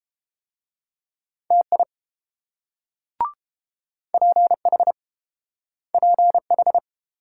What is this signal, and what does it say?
Morse code, heard as a keyed tone, about 30 wpm, 700 Hz. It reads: TI PH PH